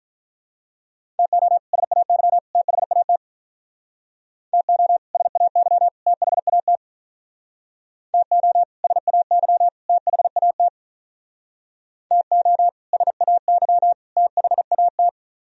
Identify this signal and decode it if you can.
Morse code, heard as a keyed tone, about 44 wpm, 700 Hz. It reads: TO SAY THAT TO SAY THAT TO SAY THAT TO SAY THAT